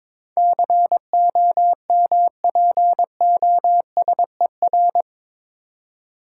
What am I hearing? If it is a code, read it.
Morse code, heard as a keyed tone, about 22 wpm, 700 Hz. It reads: COMPOSER
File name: practice / 22wpm / Eight Letter Words - 22wpm